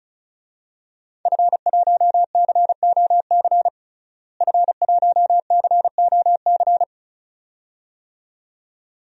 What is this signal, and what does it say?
Morse code, heard as a keyed tone, about 34 wpm, 700 Hz. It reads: F1COC F1COC
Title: DXCC-100 Call Signs - 5 Characters - Repeated 2x - 35wpm